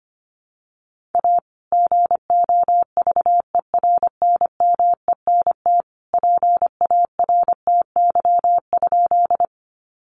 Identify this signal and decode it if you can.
Morse code, heard as a keyed tone, about 25 wpm, 700 Hz. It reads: A GOVERNMENT PARTY?
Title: Sentences from Top 1400 Words 25wpm